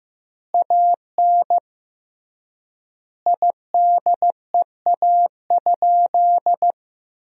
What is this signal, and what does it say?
Morse code, heard as a keyed tone, about 15 wpm, 700 Hz. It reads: AN IDEA?
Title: 3x Word-Spacing - Sentences from Top 400 Words - 15wpm